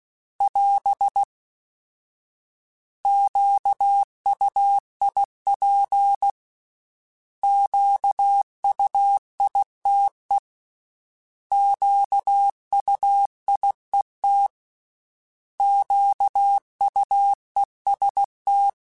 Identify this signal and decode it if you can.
Morse code, heard as a keyed tone, about 16 wpm, 800 Hz. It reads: <AS> QUIP QUITE QUIET QUEST